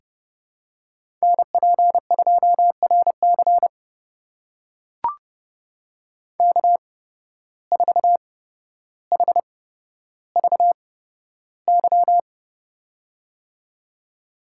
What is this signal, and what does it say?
Morse code, heard as a keyed tone, about 30 wpm, spaced as for 13 wpm, 700 Hz. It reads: NP2RC K4HVY
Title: USA Call Signs - 5 Characters - 8x Character Spacing - 30wpm